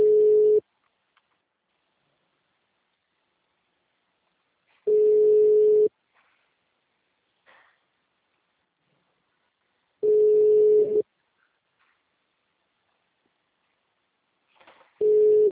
Ringback tone (Europe, 425 Hz).